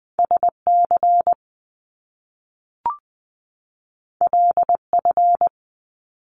Morse code, heard as a keyed tone, about 20 wpm, 700 Hz. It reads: SC LF
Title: Top Two Letter Combinations in Words 20wpm